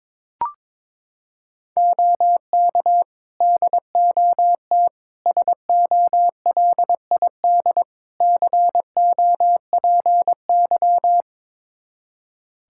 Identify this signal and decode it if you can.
Morse code, heard as a keyed tone, about 22 wpm, 700 Hz. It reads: OK DOT SOLID COPY
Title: Practice QSO - Exchange 3 - More Info - 22wpm